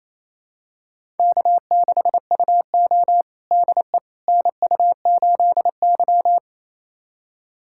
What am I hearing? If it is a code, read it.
Morse code, heard as a keyed tone, about 27 wpm, 690 Hz. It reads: K6UO DE NU8Y
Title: USA Callsign DE Callsign - 4 Characters - 28wpm